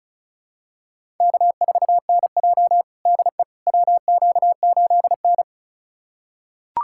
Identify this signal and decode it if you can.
Morse code, heard as a keyed tone, about 36 wpm, 700 Hz. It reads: K4NJ DE WQ8N
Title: USA Callsign DE Callsign - 4 Characters - 35wpm